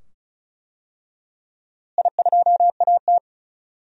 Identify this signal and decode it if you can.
Morse code, heard as a keyed tone, about 36 wpm, 700 Hz. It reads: I2AT
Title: DXCC-100 Call Signs - 4 Characters - Repeated 2x - 35wpm